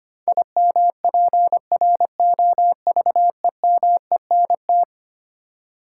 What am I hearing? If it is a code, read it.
Morse code, heard as a keyed tone, about 25 wpm, 700 Hz. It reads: IMPROVEMENT
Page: morse-code-ninja-practice-25wpm